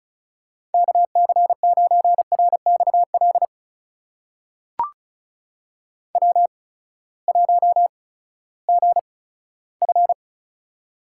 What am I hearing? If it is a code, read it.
Morse code, heard as a keyed tone, about 36 wpm, spaced as for 18 wpm, 700 Hz. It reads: KC9RXL W1GF
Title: USA Call Signs - 8x Character Spacing - 35wpm